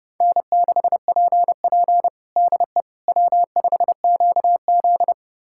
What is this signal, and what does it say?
Morse code, heard as a keyed tone, about 30 wpm, 700 Hz. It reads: N6PP DE W5QZ
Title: USA Callsign DE Callsign - 4 Characters - 30wpm